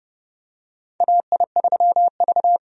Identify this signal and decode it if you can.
Morse code, heard as a keyed tone, about 30 wpm, 700 Hz. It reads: AI3V
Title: USA Call Signs - 4 Characters - Repeated 2x - 30wpm